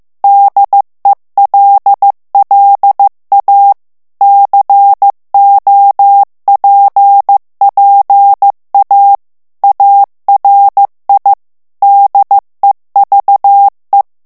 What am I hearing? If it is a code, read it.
Morse code, heard as a keyed tone, about 15 wpm, 800 Hz. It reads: DELLA COPPA ARI DEVE
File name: morse mp3 / ITALIEN texte clair 2 vitesse 14 mots minute